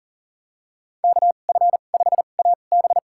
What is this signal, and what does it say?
Morse code, heard as a keyed tone, about 40 wpm, spaced as for 29 wpm, 700 Hz. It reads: KF5AB
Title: USA Call Signs - 2x Character Spacing - 40wpm